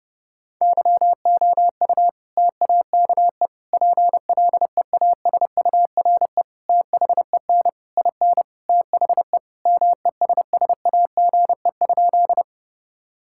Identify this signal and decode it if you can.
Morse code, heard as a keyed tone, about 30 wpm, 700 Hz. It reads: YOU TAKE PLEASURE THEN IN THE MESSAGE?
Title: Sentences from Top 1600 Words 30wpm